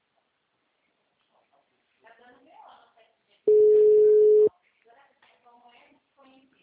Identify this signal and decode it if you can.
Ringback tone (Europe, 425 Hz), with speech.